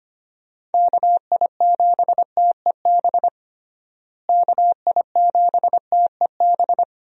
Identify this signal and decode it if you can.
Morse code, heard as a keyed tone, about 25 wpm, 700 Hz. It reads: KI7TEB KI7TEB
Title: USA Call Signs - 6 Characters - Repeated 3x - 25wpm